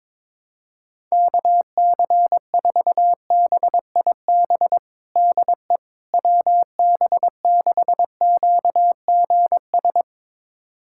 Morse code, heard as a keyed tone, about 21 wpm, 700 Hz. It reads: KC4BIB DE WB6QGS
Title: USA Callsign DE Callsign - 6 Characters - 22wpm